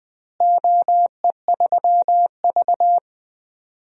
Morse code, heard as a keyed tone, about 20 wpm, 700 Hz. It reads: OE3V
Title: DXCC-100 Call Signs - 4 Characters - Repeated 3x - 20wpm